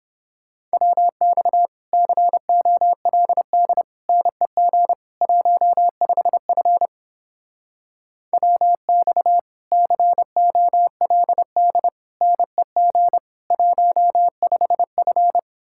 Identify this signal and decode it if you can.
Morse code, heard as a keyed tone, about 29 wpm, 700 Hz. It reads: WX COLD NEG 15F WX COLD NEG 15F